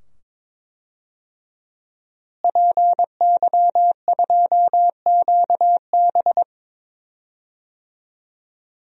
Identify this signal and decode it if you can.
Morse code, heard as a keyed tone, about 22 wpm, 700 Hz. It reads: PY2QB
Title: DXCC-100 Call Signs - 5 Characters - Repeated 2x - 22wpm